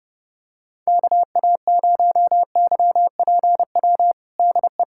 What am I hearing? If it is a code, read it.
Morse code, heard as a keyed tone, about 30 wpm, 690 Hz. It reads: KA0YPW DE
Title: USA Callsign DE Callsign - 6 Characters - 30wpm